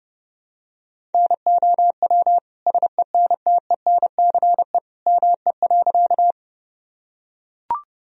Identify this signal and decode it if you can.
Morse code, heard as a keyed tone, about 32 wpm, 700 Hz. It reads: NOW SENTENCE ME.
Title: Sentences from Top 1000 Words - Review Part 1 - 30wpm